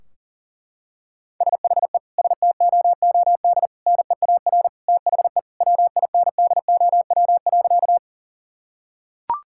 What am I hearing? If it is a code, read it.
Morse code, heard as a keyed tone, about 37 wpm, 700 Hz. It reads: SHE STOOD NEAR THE WINDOW.
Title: Sentences from Top 1600 Words 40wpm